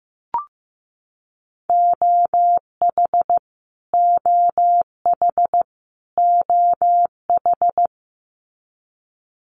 Morse code, heard as a keyed tone, about 15 wpm, 700 Hz. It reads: OH OH OH
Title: US States Repeated 3x 15wpm